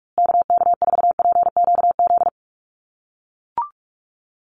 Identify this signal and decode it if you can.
Morse code, heard as a keyed tone, about 44 wpm, 700 Hz. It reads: KK4PQZ
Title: USA Call Signs Repeated 2x 45wpm